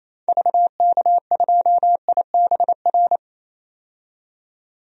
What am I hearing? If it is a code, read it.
Morse code, heard as a keyed tone, about 28 wpm, 700 Hz. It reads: VK2IBR
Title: DXCC-100 Call Signs - 6 Characters - 28wpm